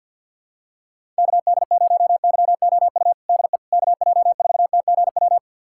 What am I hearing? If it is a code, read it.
Morse code, heard as a keyed tone, about 52 wpm, 700 Hz. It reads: KD0YOU DE KJ4TGW
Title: USA Callsign DE Callsign - 6 Characters - 50wpm